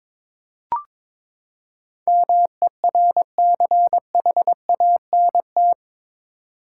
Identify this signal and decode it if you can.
Morse code, heard as a keyed tone, about 22 wpm, 700 Hz. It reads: MERCHANT
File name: practice / 22wpm / Eight Letter Words - 22wpm